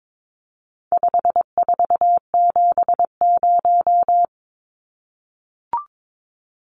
Morse code, heard as a keyed tone, about 22 wpm, 700 Hz. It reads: 5470
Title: Numbers - 4-Digit 22wpm